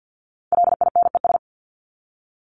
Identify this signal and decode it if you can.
Morse code, heard as a keyed tone, about 50 wpm, 700 Hz. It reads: LINES
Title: ES:S Suffix - Top 100 Words with Base Words 50wpm